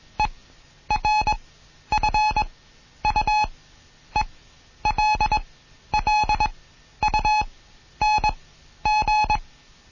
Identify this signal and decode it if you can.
Morse code, heard as a keyed tone, about 21 wpm, spaced as for 11 wpm, 850 Hz. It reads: ERFUELLUNG
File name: Morsen / Kurs A 20